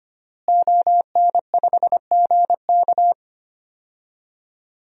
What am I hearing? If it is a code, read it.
Morse code, heard as a keyed tone, about 25 wpm, 700 Hz. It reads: ON5GK
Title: DXCC-100 Call Signs - 5 Characters - Repeated 3x - 25wpm